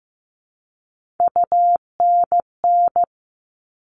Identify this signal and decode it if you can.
Morse code, heard as a keyed tone, about 15 wpm, 700 Hz. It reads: UNN
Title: All Cut Numbers - 3-Digit 15wpm